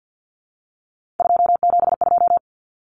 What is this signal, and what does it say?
Morse code, heard as a keyed tone, about 48 wpm, 700 Hz. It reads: 272
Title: Numbers - 3-Digit 50wpm